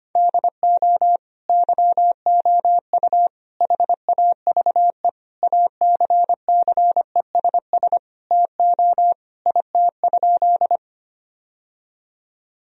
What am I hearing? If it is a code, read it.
Morse code, heard as a keyed tone, about 25 wpm, 700 Hz. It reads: DO YOU HAVE ACCESS TO IT?
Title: Sentences from Top 1000 Words - Review Part 4 - 25wpm